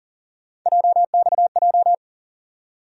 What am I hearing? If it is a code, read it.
Morse code, heard as a keyed tone, about 40 wpm, 700 Hz. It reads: JXJ
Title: USA Call Sign Postfixes - 3 Letters 40wpm